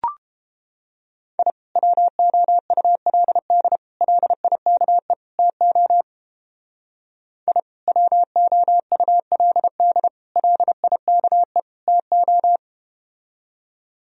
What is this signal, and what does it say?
Morse code, heard as a keyed tone, about 32 wpm, 700 Hz. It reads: I WOULD LIKE TO I WOULD LIKE TO